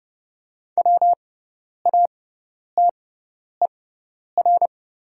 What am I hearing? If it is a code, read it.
Morse code, heard as a keyed tone, about 30 wpm, spaced as for 10 wpm, 700 Hz. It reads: WATER
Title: 6x Character Spacing - Top 500 Words - 30wpm